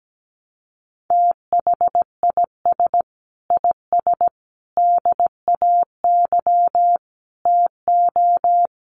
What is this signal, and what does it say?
Morse code, heard as a keyed tone, about 17 wpm, 700 Hz. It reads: THIS IS DAY TO
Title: Sets of 5 Words Encore 17wpm